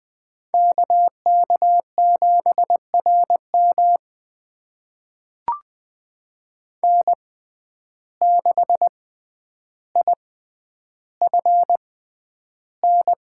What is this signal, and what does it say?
Morse code, heard as a keyed tone, about 20 wpm, spaced as for 10 wpm, 700 Hz. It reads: KK7RM N6IFN
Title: USA Call Signs - 5 Characters - 6x Character Spacing - 20wpm